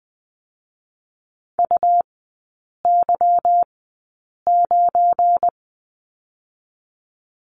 Morse code, heard as a keyed tone, about 20 wpm, spaced as for 8 wpm, 700 Hz. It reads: UY9
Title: 178 Lesson 29e - Slash - Three Characters 20wpm